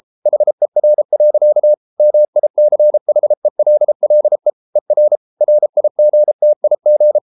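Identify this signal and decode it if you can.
Morse code, heard as a keyed tone, about 33 wpm, 600 Hz. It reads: HER. MICHELLE ER RIGTIG
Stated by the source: Morse code at 33 wpm, QRQ.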